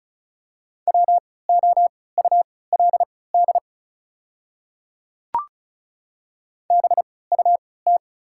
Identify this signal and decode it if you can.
Morse code, heard as a keyed tone, about 36 wpm, spaced as for 20 wpm, 700 Hz. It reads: WOULD BUT
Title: 3x Character Spacing - Top 200 Words - 35wpm